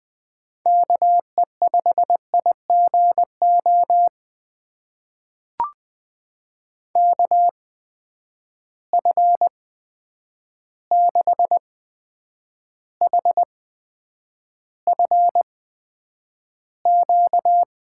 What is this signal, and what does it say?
Morse code, heard as a keyed tone, about 20 wpm, spaced as for 9 wpm, 700 Hz. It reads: KE5IGO KF6HFQ
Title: USA Call Signs - 6 Characters - 8x Character Spacing - 20wpm